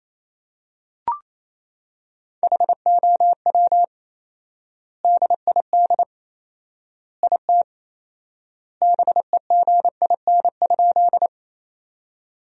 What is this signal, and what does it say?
Morse code, heard as a keyed tone, about 28 wpm, 700 Hz. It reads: HOW DID IT BEGIN?
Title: 4x Word-Spacing - Sentences from Top 300 Words - 28wpm